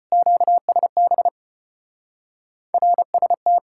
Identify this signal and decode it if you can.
Morse code, heard as a keyed tone, about 32 wpm, 700 Hz. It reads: QSB RST